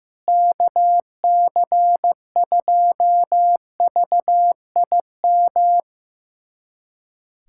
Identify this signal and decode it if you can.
Morse code, heard as a keyed tone, about 15 wpm, 700 Hz. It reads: KC2VIM
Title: USA Callsign DE Callsign - 6 Characters - 15wpm